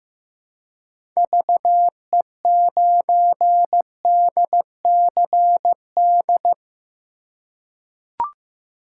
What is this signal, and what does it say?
Morse code, heard as a keyed tone, about 15 wpm, 700 Hz. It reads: VE9DCD
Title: K1USN SST QSO Practice - 15wpm